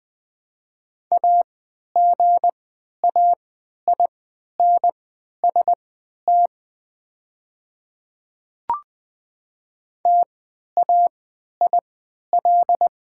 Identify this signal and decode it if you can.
Morse code, heard as a keyed tone, about 20 wpm, spaced as for 11 wpm, 700 Hz. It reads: AGAINST TAIL
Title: 3x Character Spacing - Top 500 Words - 20wpm